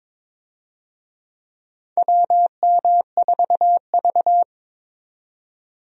Morse code, heard as a keyed tone, about 21 wpm, 700 Hz. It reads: WM4V
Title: USA Call Signs - 4 Characters - 22wpm